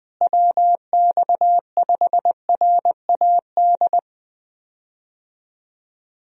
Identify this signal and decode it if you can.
Morse code, heard as a keyed tone, about 21 wpm, 700 Hz. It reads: WX5RAD
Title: USA Call Signs - 6 Characters - Repeated 3x - 20wpm